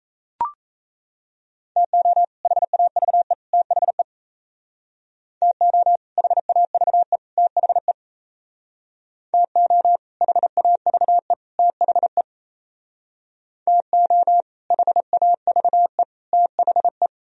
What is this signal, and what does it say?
Morse code, heard as a keyed tone, about 36 wpm, 700 Hz. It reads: TO HAVE THE TO HAVE THE TO HAVE THE TO HAVE THE